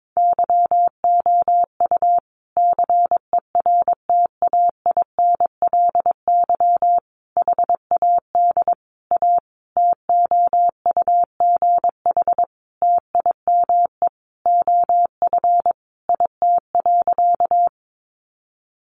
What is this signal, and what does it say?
Morse code, heard as a keyed tone, about 22 wpm, 700 Hz. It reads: YOU CERTAINLY HAD A TOUGH TIME OF IT.
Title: Sentences from Top 1200 Words 22wpm